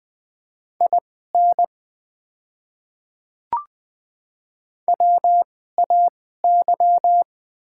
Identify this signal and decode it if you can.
Morse code, heard as a keyed tone, about 21 wpm, spaced as for 15 wpm, 700 Hz. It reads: IN WAY
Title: 2x Character Spacing - Top 100 Words - 20wpm